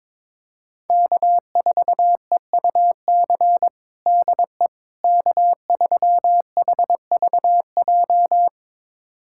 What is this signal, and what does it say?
Morse code, heard as a keyed tone, about 22 wpm, 700 Hz. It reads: K4EUC DE K3HVJ
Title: USA Callsign DE Callsign - 5 Characters - 22wpm